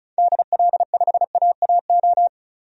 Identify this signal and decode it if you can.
Morse code, heard as a keyed tone, about 34 wpm, 700 Hz. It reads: DL5AAO
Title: DXCC-100 Call Signs - 6 Characters - 35wpm